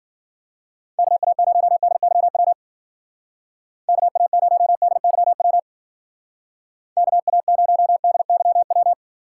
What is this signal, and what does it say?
Morse code, heard as a keyed tone, about 52 wpm, 700 Hz. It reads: KA0DYW KA0DYW KA0DYW